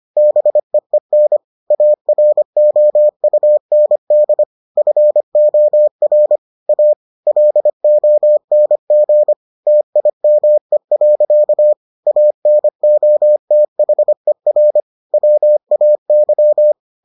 Morse code, heard as a keyed tone, about 25 wpm, 600 Hz. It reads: BEEN AROUND FOR A LONG TIME. ANOTHER WAY